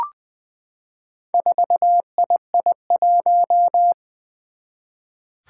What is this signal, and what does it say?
Morse code, heard as a keyed tone, about 20 wpm, 700 Hz. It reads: 4II1
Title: Four Challenging Letter-Number Combinations 20wpm